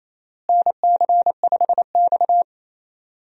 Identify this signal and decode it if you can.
Morse code, heard as a keyed tone, about 28 wpm, 700 Hz. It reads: NC5X
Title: USA Call Signs - 4 Characters - Repeated 3x - 28wpm